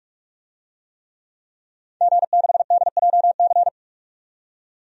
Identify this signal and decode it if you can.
Morse code, heard as a keyed tone, about 46 wpm, 700 Hz. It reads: G6DJC
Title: DXCC-100 Call Signs - 5 Characters - 45wpm